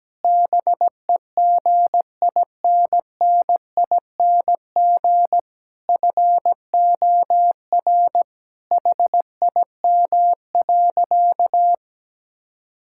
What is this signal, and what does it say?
Morse code, heard as a keyed tone, about 17 wpm, 700 Hz. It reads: BEGINNING FOR HIM.
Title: Sentences from Top 1000 Words - Review Part 5 - 17wpm